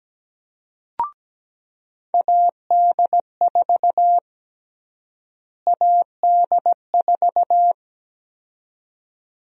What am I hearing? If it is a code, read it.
Morse code, heard as a keyed tone, about 17 wpm, 700 Hz. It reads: AD4 AD4
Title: USA Call Sign Prefixes - Repeated 2x - 17wpm